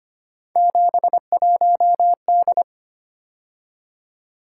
Morse code, heard as a keyed tone, about 25 wpm, 700 Hz. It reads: 71D
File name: Three Letter-Number Combinations - 25wpm